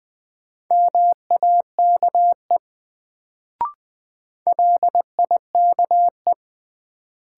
Morse code, heard as a keyed tone, about 20 wpm, 700 Hz. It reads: MAKE LIKE
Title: Sets of 2 Words - New Word Drill - 20wpm